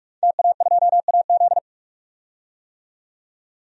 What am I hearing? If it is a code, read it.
Morse code, heard as a keyed tone, about 41 wpm, 700 Hz. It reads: TA2AZ
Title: DXCC-100 Call Signs - 5 Characters - 45wpm